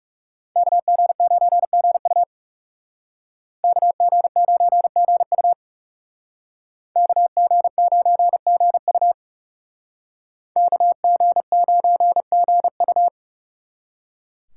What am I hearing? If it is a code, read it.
Morse code, heard as a keyed tone, about 36 wpm, 700 Hz. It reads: KG9GU KG9GU KG9GU KG9GU
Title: Speed Racing - USA 2x2 Call Signs 30wpm